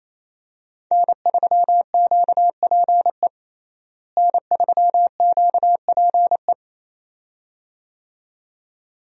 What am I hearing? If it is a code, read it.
Morse code, heard as a keyed tone, about 27 wpm, 700 Hz. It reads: N3QPE N3QPE
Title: USA Call Signs - 5 Characters - Repeated 2x - 28wpm